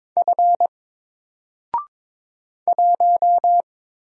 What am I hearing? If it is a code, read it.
Morse code, heard as a keyed tone, about 22 wpm, 700 Hz. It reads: F 1
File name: Single-Letter-Number-22wpm